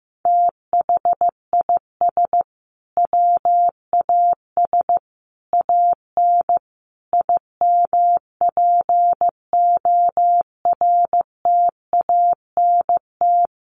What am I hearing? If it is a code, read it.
Morse code, heard as a keyed tone, about 15 wpm, 700 Hz. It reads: THIS WAS AN IMPORTANT
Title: Sentences from Top 1300 Words 15wpm